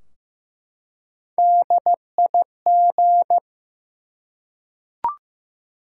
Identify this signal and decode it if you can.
Morse code, heard as a keyed tone, about 15 wpm, 700 Hz. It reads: DIG
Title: Three Letter Words - 15wpm